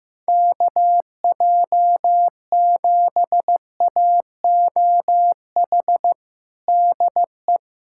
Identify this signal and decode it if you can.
Morse code, heard as a keyed tone, about 15 wpm, 700 Hz. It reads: KJ7AOH DE
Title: USA Callsign DE Callsign - 6 Characters - 15wpm